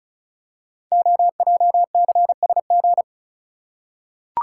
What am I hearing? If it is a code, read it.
Morse code, heard as a keyed tone, about 36 wpm, 690 Hz. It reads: OJCSG
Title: Code Groups - 5 Letters - 35wpm